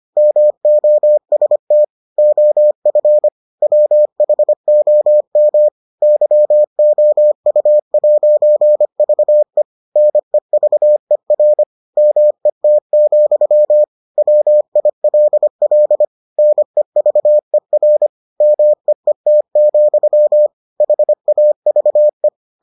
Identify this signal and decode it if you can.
Morse code, heard as a keyed tone, about 25 wpm, 600 Hz. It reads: MOST OF WHOM YOU'VE NEVER MET, WILL NEVER MEET, HAVE